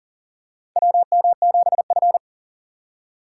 Morse code, heard as a keyed tone, about 38 wpm, 700 Hz. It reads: WM7F
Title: USA Call Signs - 4 Characters - 2x Character Spacing - 40wpm